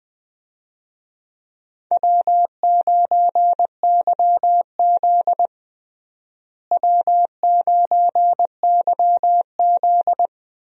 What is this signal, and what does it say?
Morse code, heard as a keyed tone, about 21 wpm, 700 Hz. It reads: W9YZ W9YZ
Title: USA Call Signs - 4 Characters - Repeated 3x - 20wpm